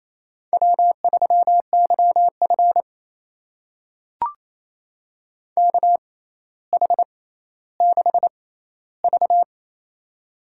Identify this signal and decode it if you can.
Morse code, heard as a keyed tone, about 27 wpm, spaced as for 14 wpm, 700 Hz. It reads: W3YF KH6V
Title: USA Call Signs - 4 Characters - 6x Character Spacing - 28wpm